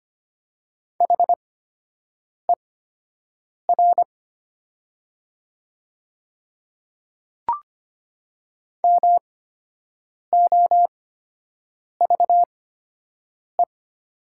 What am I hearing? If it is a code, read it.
Morse code, heard as a keyed tone, about 25 wpm, spaced as for 7 wpm, 700 Hz. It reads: HER MOVE
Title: 8x Character Spacing - Top 200 Words - 25wpm